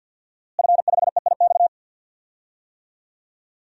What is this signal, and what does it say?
Morse code, heard as a keyed tone, about 52 wpm, 700 Hz. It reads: F5EIX
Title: DXCC-100 Call Signs - 5 Characters - Repeated 2x - 50wpm